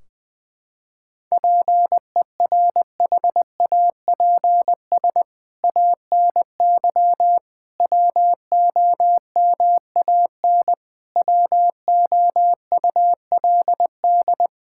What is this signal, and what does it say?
Morse code, heard as a keyed tone, about 21 wpm, 700 Hz. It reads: PERHAPS ANY WOMAN WOULD
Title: Sentences from Top 1600 Words 20wpm